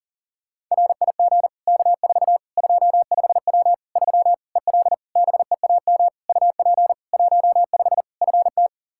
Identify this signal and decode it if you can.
Morse code, heard as a keyed tone, about 40 wpm, 700 Hz. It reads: RIG K4 25W 3 EL BEAM UP 15 FT
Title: Practice QSO - Exchange 2 - Info - 40wpm